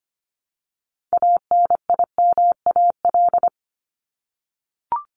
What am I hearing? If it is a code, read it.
Morse code, heard as a keyed tone, about 25 wpm, 700 Hz. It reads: ANIMAL
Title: ES:S Suffix - Top 100 Words with Base Words 25wpm